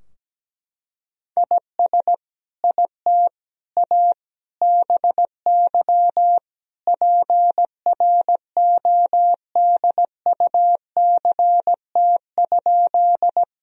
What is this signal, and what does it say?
Morse code, heard as a keyed tone, about 17 wpm, 700 Hz. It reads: IS IT A BY PRODUCT?
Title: Sentences from Top 1000 Words - Review Part 2 - 17wpm